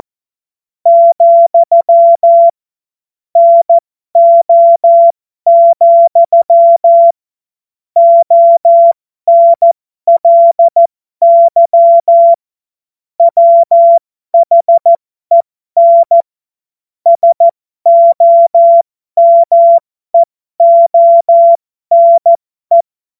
Morse code, heard as a keyed tone, about 14 wpm, spaced as for 12 wpm, 680 Hz. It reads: , NO, ONLY WHEN SOMEONE
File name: Morsekurs / Witze